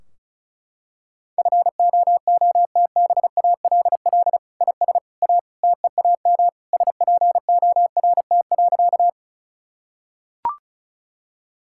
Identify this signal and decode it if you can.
Morse code, heard as a keyed tone, about 34 wpm, 700 Hz. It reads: FOOTBALL IS A TEAM SPORT.
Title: Sentences from Top 1600 Words 35wpm